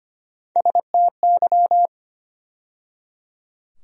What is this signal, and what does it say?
Morse code, heard as a keyed tone, about 25 wpm, 700 Hz. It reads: STY
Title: USA Call Sign Postfixes - 3 Letters 25wpm